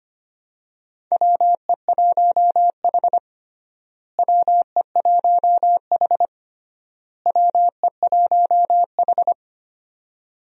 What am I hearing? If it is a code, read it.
Morse code, heard as a keyed tone, about 25 wpm, 700 Hz. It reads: WE1H WE1H WE1H
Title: USA Call Signs - 4 Characters - Repeated 3x - 25wpm